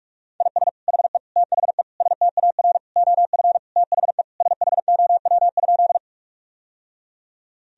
Morse code, heard as a keyed tone, about 46 wpm, 700 Hz. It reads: IS HE THE STAR OF THE SHOW?
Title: Sentences from Top 1000 Words - Review Part 2 - 45wpm